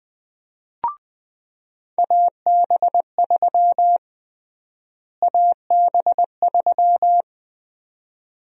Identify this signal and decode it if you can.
Morse code, heard as a keyed tone, about 20 wpm, 700 Hz. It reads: AB3 AB3
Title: USA Call Sign Prefixes - Repeated 2x - 20wpm